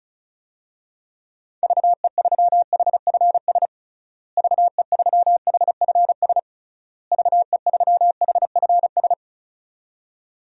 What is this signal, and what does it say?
Morse code, heard as a keyed tone, about 36 wpm, 700 Hz. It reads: VE3HFS VE3HFS VE3HFS
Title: DXCC-100 Call Signs - 6 Characters - Repeated 3x - 35wpm